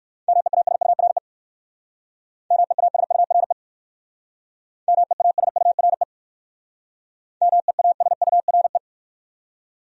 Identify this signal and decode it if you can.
Morse code, heard as a keyed tone, about 56 wpm, 700 Hz. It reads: MEASURE MEASURE MEASURE MEASURE